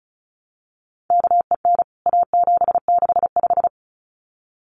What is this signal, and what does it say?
Morse code, heard as a keyed tone, about 34 wpm, 700 Hz. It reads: KEN A765
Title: CWT Exchange 35wpm